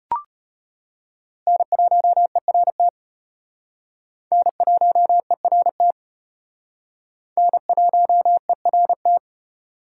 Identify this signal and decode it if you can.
Morse code, heard as a keyed tone, about 34 wpm, 700 Hz. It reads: N1ERT N1ERT N1ERT